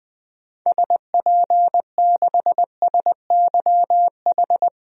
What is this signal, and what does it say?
Morse code, heard as a keyed tone, about 20 wpm, 700 Hz. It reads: SP6SYH
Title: DXCC-100 Call Signs - 6 Characters - 20wpm